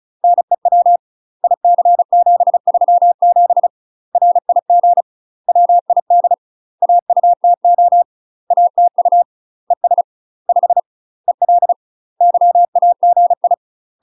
Morse code, heard as a keyed tone, about 34 wpm, 700 Hz. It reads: NEW IC737 RIG WID AUTO ATU ES 5 EL YAGI
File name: UK MASTER QSOs TRAINING / 35WPM / UK MASTER QSOs 0005